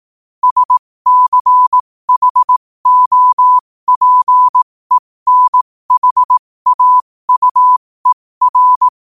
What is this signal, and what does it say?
Morse code, heard as a keyed tone, about 18 wpm, spaced as for 15 wpm, 1000 Hz. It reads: SCHOPENHAUER
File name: qotdfiles / 15wpm